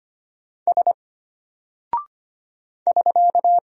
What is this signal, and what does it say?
Morse code, heard as a keyed tone, about 25 wpm, 700 Hz. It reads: S <SK>
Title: Single Character (Letter:Number:Punctuation:Prosign) Alt 25wpm